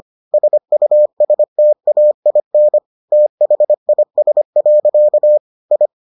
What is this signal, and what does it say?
Morse code, heard as a keyed tone, about 25 wpm, 600 Hz. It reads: SUSTAIN THIS. I